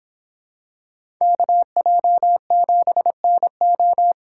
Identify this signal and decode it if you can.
Morse code, heard as a keyed tone, about 26 wpm, 700 Hz. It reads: KJ7NO